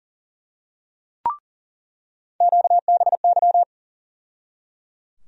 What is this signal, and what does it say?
Morse code, heard as a keyed tone, about 37 wpm, 700 Hz. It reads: QBY
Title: USA Call Sign Postfixes - 3 Letters 40wpm